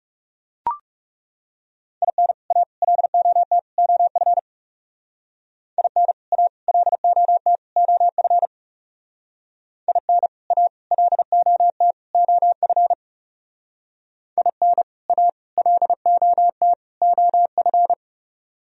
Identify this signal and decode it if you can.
Morse code, heard as a keyed tone, about 36 wpm, 700 Hz. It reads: IN A LOT OF IN A LOT OF IN A LOT OF IN A LOT OF
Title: Speed-Racing - Sets of 4 Words 30wpm